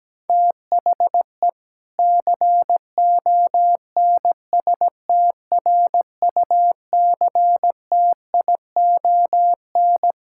Morse code, heard as a keyed tone, about 17 wpm, 700 Hz. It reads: THE CONSTRUCTION
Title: Sentences from Top 1600 Words 17wpm